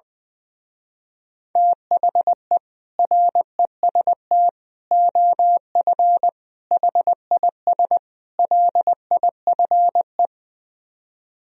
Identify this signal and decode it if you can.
Morse code, heard as a keyed tone, about 20 wpm, 700 Hz. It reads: THE REST OF HIS LIFE
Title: Speed-Racing - Sets of 5 Words 20wpm